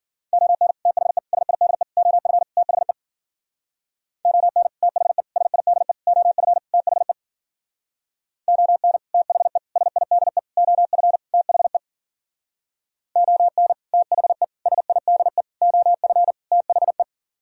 Morse code, heard as a keyed tone, about 51 wpm, 700 Hz. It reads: ON THE SIDE OF THE ON THE SIDE OF THE ON THE SIDE OF THE ON THE SIDE OF THE